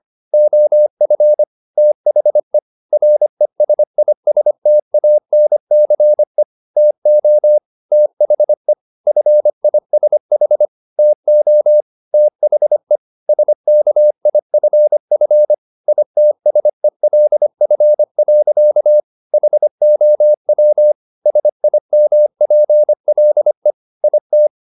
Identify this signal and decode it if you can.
Morse code, heard as a keyed tone, about 25 wpm, 600 Hz. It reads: OF THE RESISTANCE TO THE FISH TO THE SKIFF ITSELF. HOW SIMPLE IT